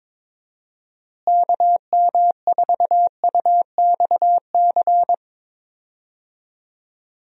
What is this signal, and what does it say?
Morse code, heard as a keyed tone, about 22 wpm, 700 Hz. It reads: KM4UXC
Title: USA Call Signs - 6 Characters - Repeated 2x - 22wpm